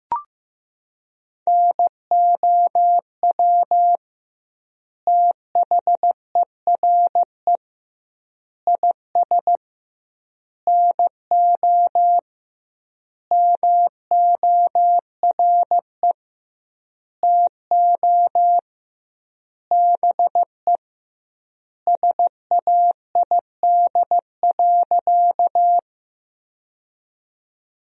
Morse code, heard as a keyed tone, about 15 wpm, 700 Hz. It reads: NOW THERE IS NO MORE TO BE SAID.
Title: 2x Word-Spacing - Sentences from Top 100 Words - 15wpm